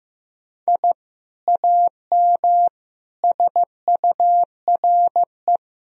Morse code, heard as a keyed tone, about 15 wpm, 700 Hz. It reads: I AM SURE
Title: Sentences from Top 1000 Words - Review Part 2 - 15wpm